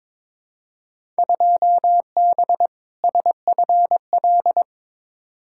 Morse code, heard as a keyed tone, about 22 wpm, 700 Hz. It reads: 2B SFL
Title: Field Day Exchanges - 22wpm